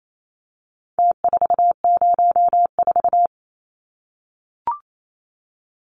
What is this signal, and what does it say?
Morse code, heard as a keyed tone, about 28 wpm, 700 Hz. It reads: T404
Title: Numbers - Contest Style - 4-Digit 28wpm